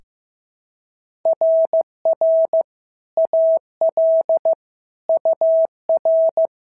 Morse code, heard as a keyed tone, about 14 wpm, 650 Hz. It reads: RR AL UR
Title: SOTA QSO Practice 15wpm